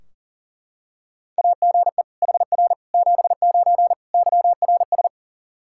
Morse code, heard as a keyed tone, about 40 wpm, 700 Hz. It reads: AGE HR 79 YRS
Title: Practice QSO - Exchange 3 - More Info - 40wpm